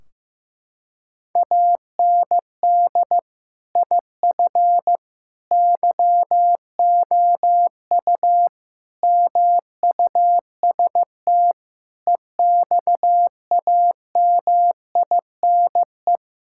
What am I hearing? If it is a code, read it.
Morse code, heard as a keyed tone, about 15 wpm, 700 Hz. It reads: AND IF YOU MUST EXAMINE
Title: Sentences from Top 1600 Words 15wpm